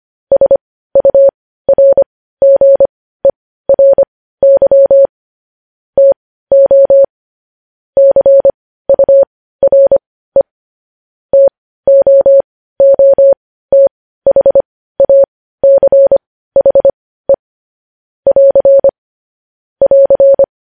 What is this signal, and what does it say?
Morse code, heard as a keyed tone, about 25 wpm, spaced as for 15 wpm, 550 Hz. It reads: SURGERY TO CURE TOOTHACHE <AR> <AR>
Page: news headlines in morse code at 15 wpm